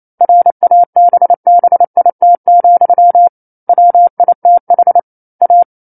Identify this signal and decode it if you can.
Morse code, heard as a keyed tone, about 28 wpm, 700 Hz. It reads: RABBIT, WITH A
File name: 09-Chapter-09 28 WPM